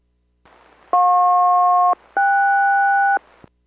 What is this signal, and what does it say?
Aircraft SELCAL, heard as two pairs of tones. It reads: JP-KS